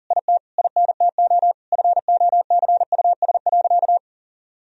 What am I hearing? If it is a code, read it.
Morse code, heard as a keyed tone, about 40 wpm, 700 Hz. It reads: IT INTO FOCUS.
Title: Sentences from Top 1000 Words - Review Part 4 - 40wpm